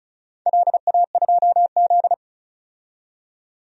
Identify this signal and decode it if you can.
Morse code, heard as a keyed tone, about 36 wpm, 700 Hz. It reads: LA2Z
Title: DXCC-100 Call Signs - 4 Characters - 35wpm